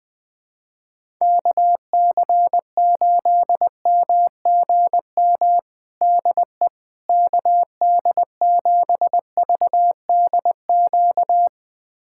Morse code, heard as a keyed tone, about 20 wpm, 700 Hz. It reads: KC8MGM DE KD7VDQ
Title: USA Callsign DE Callsign - 6 Characters - 20wpm